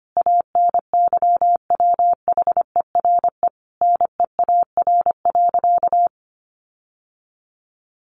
Morse code, heard as a keyed tone, about 25 wpm, 700 Hz. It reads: ANYWHERE NEAR.
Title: Sentences from Top 1500 Words 25wpm